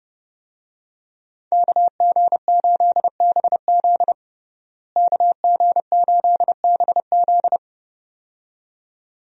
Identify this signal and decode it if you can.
Morse code, heard as a keyed tone, about 32 wpm, 700 Hz. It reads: KG8BZ KG8BZ
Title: USA Call Signs - 5 Characters - Repeated 2x - 30wpm